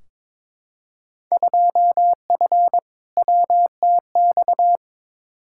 Morse code, heard as a keyed tone, about 22 wpm, 700 Hz. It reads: 2F WTX
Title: Field Day Exchanges - 22wpm